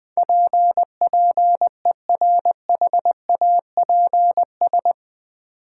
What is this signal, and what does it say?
Morse code, heard as a keyed tone, about 20 wpm, 700 Hz. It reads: PPERHAPS
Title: Sentences from Top 1100 Words - New Word Drill Alt - 20wpm